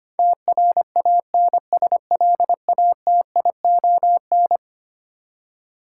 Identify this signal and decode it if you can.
Morse code, heard as a keyed tone, about 25 wpm, 700 Hz. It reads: TRANSLATION
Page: morse-code-ninja-practice-25wpm